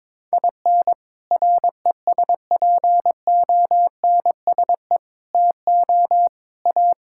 Morse code, heard as a keyed tone, about 21 wpm, 700 Hz. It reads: IN RESPONSE TO A
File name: practice / 22wpm / Sets of 6 Words - 22wpm